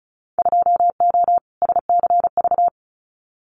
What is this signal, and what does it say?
Morse code, heard as a keyed tone, about 34 wpm, 700 Hz. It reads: 2O SCV
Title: Winter Field Day Exchanges Encore 35wpm